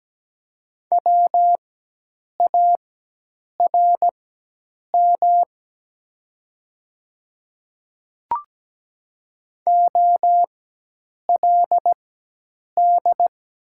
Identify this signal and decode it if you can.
Morse code, heard as a keyed tone, about 17 wpm, spaced as for 8 wpm, 700 Hz. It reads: WARM OLD
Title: 4x Character Spacing - Top 500 Words - 17wpm